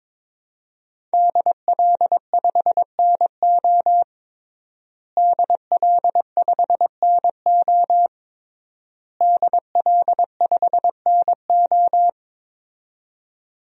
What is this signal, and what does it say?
Morse code, heard as a keyed tone, about 22 wpm, 700 Hz. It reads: DL5NO DL5NO DL5NO
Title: DXCC-100 Call Signs - 5 Characters - Repeated 3x - 22wpm